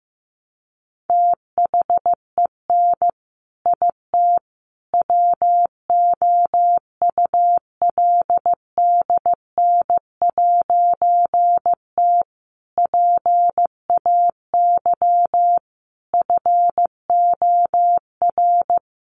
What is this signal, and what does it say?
Morse code, heard as a keyed tone, about 15 wpm, 700 Hz. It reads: THEN IT WOULDN'T PAY FOR
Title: A Sound Of Thunder 15wpm